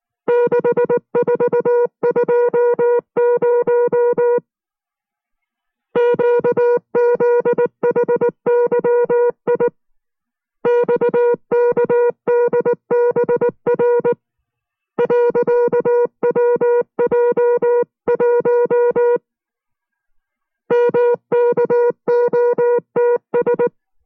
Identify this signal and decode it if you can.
Morse code, heard as a keyed tone, about 19 wpm, 480 Hz. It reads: 6420 QZHYI XKDBR .WJ1 MKOTS